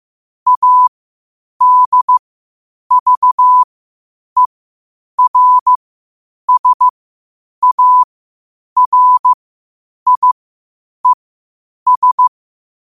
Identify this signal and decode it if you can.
Morse code, heard as a keyed tone, about 15 wpm, spaced as for 8 wpm, 1000 Hz. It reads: ADVERSARIES
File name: qotdfiles / 8wpm